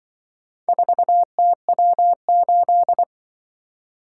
Morse code, heard as a keyed tone, about 24 wpm, 700 Hz. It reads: 4TW8
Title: Speed-Racing - Four Letter-Number Combinations - 20wpm